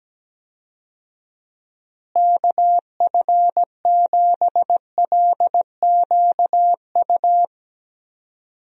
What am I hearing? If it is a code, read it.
Morse code, heard as a keyed tone, about 17 wpm, 700 Hz. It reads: KF7LQU
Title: USA Call Signs - 6 Characters - Repeated 2x - 17wpm